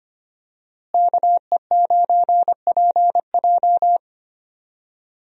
Morse code, heard as a keyed tone, about 25 wpm, 700 Hz. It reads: KE9PJ
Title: USA Call Signs - 5 Characters - Repeated 2x - 25wpm